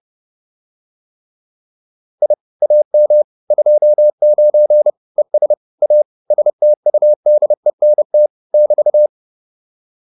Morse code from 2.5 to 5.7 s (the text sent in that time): AM 29 ES